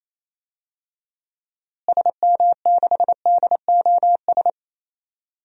Morse code, heard as a keyed tone, about 28 wpm, 700 Hz. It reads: SM6DOS
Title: DXCC-100 Call Signs - 6 Characters - Repeated 2x - 28wpm